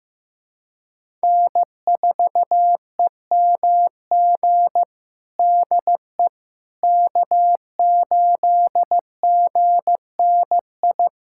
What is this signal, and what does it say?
Morse code, heard as a keyed tone, about 15 wpm, 700 Hz. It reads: N4EMG DE K8GNI
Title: USA Callsign DE Callsign - 5 Characters - 15wpm